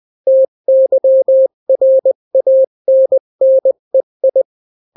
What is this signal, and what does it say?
Morse code, heard as a keyed tone, about 20 wpm, 530 Hz. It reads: TYRANNEI
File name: daily / 18wpm